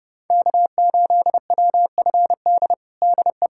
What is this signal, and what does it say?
Morse code, heard as a keyed tone, about 30 wpm, 700 Hz. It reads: K8WFD DE
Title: USA Callsign DE Callsign - 5 Characters - 30wpm